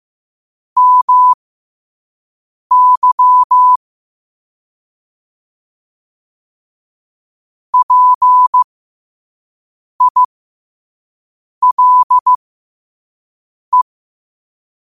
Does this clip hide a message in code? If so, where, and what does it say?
Yes: Morse code from 0.8 to 13.8 s, heard as a keyed tone, about 15 wpm, spaced as for 5 wpm, 1000 Hz. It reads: MY PILE